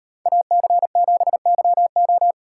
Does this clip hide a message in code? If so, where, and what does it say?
Yes: Morse code from 0.3 to 2.3 s, heard as a keyed tone, about 38 wpm, 700 Hz. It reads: AC7YO